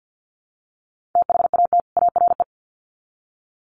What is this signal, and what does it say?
Morse code, heard as a keyed tone, about 50 wpm, 700 Hz. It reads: THAT ARE
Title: Sets of 2 Words 50wpm